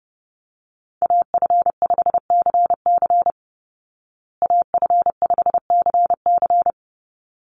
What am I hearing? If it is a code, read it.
Morse code, heard as a keyed tone, about 28 wpm, 700 Hz. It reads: AF5CC AF5CC
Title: Call Signs Repeated 3x - 30wpm